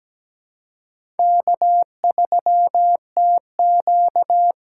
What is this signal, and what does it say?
Morse code, heard as a keyed tone, about 17 wpm, 700 Hz. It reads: K3TQ
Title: USA Call Signs - 4 Characters - 2x Character Spacing - 17wpm